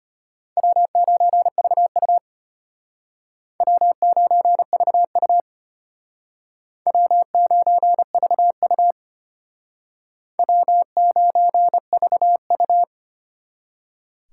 Morse code, heard as a keyed tone, about 32 wpm, 700 Hz. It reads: W9VU W9VU W9VU W9VU